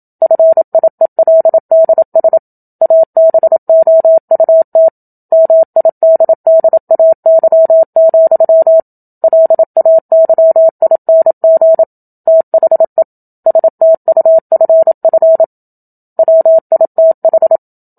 Morse code, heard as a keyed tone, about 27 wpm, 650 Hz. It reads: FIELDS ABOUT MIDDAY, LAYING THE STUFF WITH
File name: Book Two / 01-Chapter 1 - 27 WPM